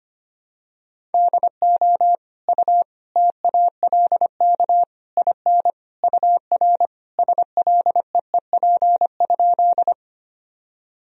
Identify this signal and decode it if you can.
Morse code, heard as a keyed tone, about 25 wpm, 700 Hz. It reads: DO U TALK IN UR SLEEP?
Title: Sentences from Top 500 Words and Abbreviations 25wpm